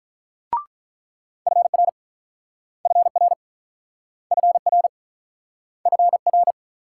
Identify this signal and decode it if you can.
Morse code, heard as a keyed tone, about 48 wpm, 700 Hz. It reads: FR FR FR FR